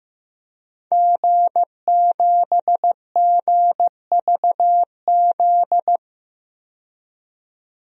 Morse code, heard as a keyed tone, about 15 wpm, 700 Hz. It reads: G7GVZ
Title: DXCC-100 Call Signs - 5 Characters - 15wpm